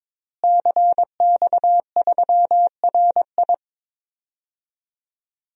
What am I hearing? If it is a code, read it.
Morse code, heard as a keyed tone, about 22 wpm, 700 Hz. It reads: CX3RI
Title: DXCC-100 Call Signs - 5 Characters - Repeated 3x - 22wpm